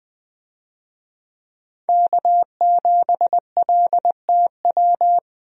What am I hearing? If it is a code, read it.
Morse code, heard as a keyed tone, about 20 wpm, 700 Hz. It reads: K7LTW
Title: USA Call Signs - 5 Characters - Repeated 2x - 20wpm